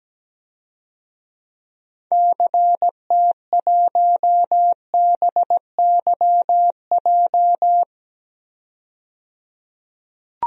Morse code, heard as a keyed tone, about 17 wpm, 700 Hz. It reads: CT1BYJ
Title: DXCC-100 Call Signs - 6 Characters - 17wpm